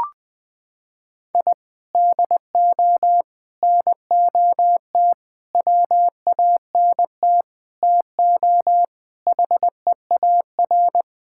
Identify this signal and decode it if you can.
Morse code, heard as a keyed tone, about 21 wpm, 700 Hz. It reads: I DO NOT WANT TO HEAR
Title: Sentences with 5-Word Phrases - 20wpm